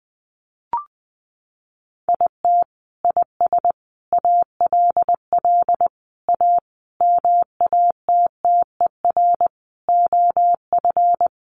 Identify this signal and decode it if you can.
Morse code, heard as a keyed tone, about 20 wpm, 700 Hz. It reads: IT IS ALL A MATTER OF
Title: Sentences from Top 1300 Words 20wpm